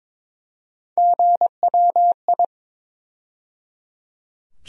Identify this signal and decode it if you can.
Morse code, heard as a keyed tone, about 22 wpm, 700 Hz. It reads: GWI
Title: USA Call Sign Postfixes - 3 Letters 22wpm